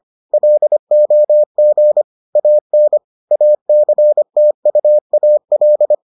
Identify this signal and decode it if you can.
Morse code, heard as a keyed tone, about 25 wpm, 600 Hz. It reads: LOG AN ACTUAL